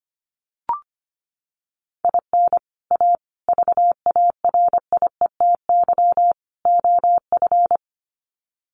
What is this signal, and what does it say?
Morse code, heard as a keyed tone, about 25 wpm, 700 Hz. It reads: IN A VARIETY OF
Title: Sets of 4 Words 25wpm